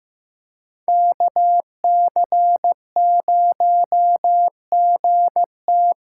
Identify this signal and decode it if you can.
Morse code, heard as a keyed tone, about 15 wpm, 700 Hz. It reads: KC0GT
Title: USA Call Signs - 6 Characters - 8x Character Spacing - 15wpm